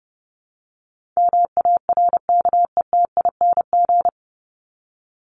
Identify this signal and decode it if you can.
Morse code, heard as a keyed tone, about 30 wpm, 700 Hz. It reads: MARKETING
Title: Top 751-1000 Words 30wpm